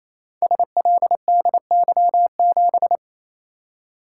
Morse code, heard as a keyed tone, about 28 wpm, 700 Hz. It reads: SLDY7
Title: Code Groups - 5 Letter:Numbers - 28wpm